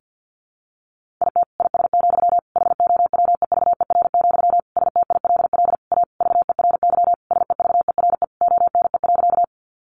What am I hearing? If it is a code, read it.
Morse code, heard as a keyed tone, about 50 wpm, 700 Hz. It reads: IT IS, HOWEVER, STILL A VERY SEVERE ONE.
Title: Sentences from Top 1400 Words 50wpm